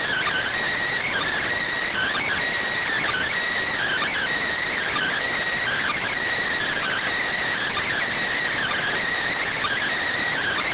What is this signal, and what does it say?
Emergency Alert System (SAME) tones (header bursts), heard as harsh buzzing bursts. It is heard from 0.6 to 10.5 s.